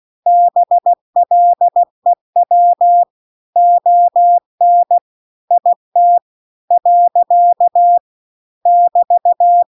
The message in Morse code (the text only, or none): BLEW ON IT . <BT>